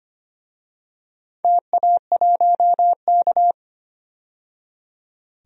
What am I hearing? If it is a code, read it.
Morse code, heard as a keyed tone, about 25 wpm, 700 Hz. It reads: TA1K
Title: DXCC-100 Call Signs - 4 Characters - 25wpm